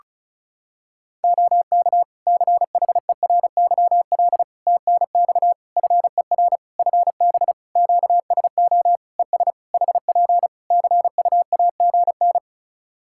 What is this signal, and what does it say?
Morse code, heard as a keyed tone, about 34 wpm, 700 Hz. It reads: OK CHERYL TNX FER FB QSO ES HP CUAGN
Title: Practice QSO - Exchange 4 - Ending - 35wpm